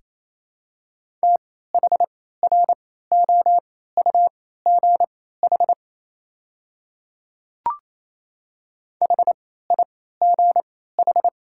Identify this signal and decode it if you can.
Morse code, heard as a keyed tone, about 28 wpm, spaced as for 16 wpm, 700 Hz. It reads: THROUGH HIGH
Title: 3x Character Spacing - Top 200 Words - 28wpm